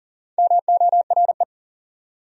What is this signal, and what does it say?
Morse code, heard as a keyed tone, about 40 wpm, 700 Hz. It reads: MORE
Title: Top 100 Words Repeated 40wpm